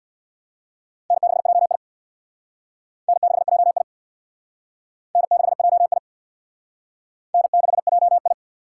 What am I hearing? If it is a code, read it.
Morse code, heard as a keyed tone, about 65 wpm, 700 Hz. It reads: N6JI N6JI N6JI N6JI